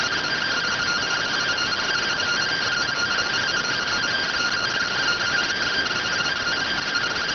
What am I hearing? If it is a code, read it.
Two-tone teleprinter signal, about 170 Hz shift.